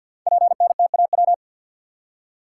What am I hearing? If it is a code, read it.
Morse code, heard as a keyed tone, about 52 wpm, 700 Hz. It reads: PNTAW
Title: Code Groups - 5 Letters - 50wpm